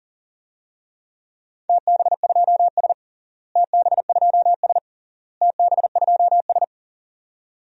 Morse code, heard as a keyed tone, about 38 wpm, 700 Hz. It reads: TB2S TB2S TB2S
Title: DXCC-100 Call Signs - 4 Characters - Repeated 3x - 40wpm